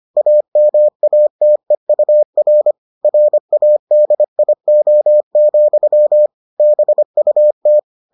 Morse code, heard as a keyed tone, about 25 wpm, 600 Hz. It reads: AMATEUR RADIO, BUT